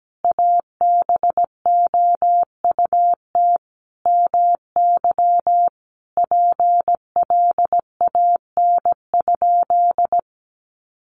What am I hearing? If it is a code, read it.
Morse code, heard as a keyed tone, about 17 wpm, 700 Hz. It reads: ABOUT MY PLAN?
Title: Sentences from Top 500 Words 17wpm